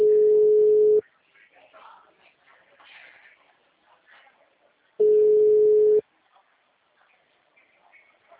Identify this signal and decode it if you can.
Ringback tone (Europe, 425 Hz).